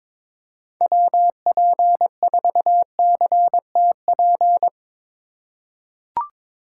Morse code, heard as a keyed tone, about 22 wpm, 700 Hz. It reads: WP4CTP
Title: USA Call Signs - 3x Character Spacing - 22wpm